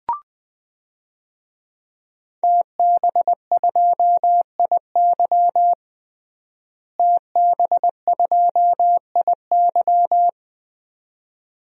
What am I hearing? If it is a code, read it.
Morse code, heard as a keyed tone, about 20 wpm, 700 Hz. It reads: TB2IY TB2IY
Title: DXCC-100 Call Signs - 5 Characters - Repeated 2x - 20wpm